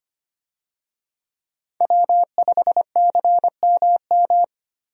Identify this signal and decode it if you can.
Morse code, heard as a keyed tone, about 25 wpm, 700 Hz. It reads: W5CMM
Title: USA Call Signs - 5 Characters - Repeated 2x - 25wpm